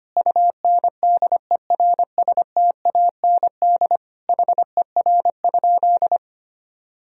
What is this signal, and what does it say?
Morse code, heard as a keyed tone, about 25 wpm, 700 Hz. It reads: UNDERSTAND HER?
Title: Sentences from Top 1000 Words - Review Part 2 - 25wpm